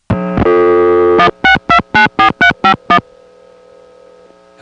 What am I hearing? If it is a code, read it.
DTMF, heard as telephone touch tones, tones of about 100 ms. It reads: 16294351
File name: phonetrips / SmyrnaGA3HQ